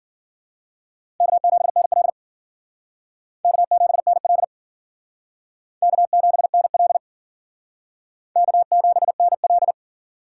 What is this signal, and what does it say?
Morse code, heard as a keyed tone, about 53 wpm, 700 Hz. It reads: K7NL K7NL K7NL K7NL